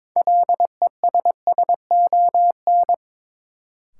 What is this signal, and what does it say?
Morse code, heard as a keyed tone, about 22 wpm, 700 Hz. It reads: LESSON
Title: Sentences from Top 1200 Words - New Word Drill - 22wpm